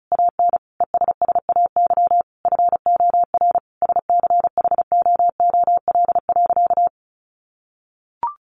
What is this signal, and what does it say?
Morse code, heard as a keyed tone, about 34 wpm, 700 Hz. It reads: AN ESSAY FOR SCHOOL.
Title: Sentences from Top 1500 Words 35wpm